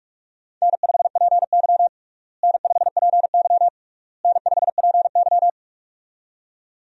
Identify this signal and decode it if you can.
Morse code, heard as a keyed tone, about 47 wpm, 700 Hz. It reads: N5PY N5PY N5PY
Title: USA Call Signs - 4 Characters - Repeated 3x - 45wpm